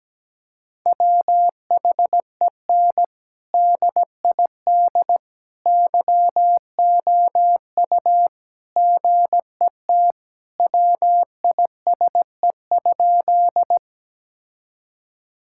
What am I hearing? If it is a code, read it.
Morse code, heard as a keyed tone, about 17 wpm, 700 Hz. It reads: WHEN DID YOU GET WISE?
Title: Sentences from Top 1600 Words 17wpm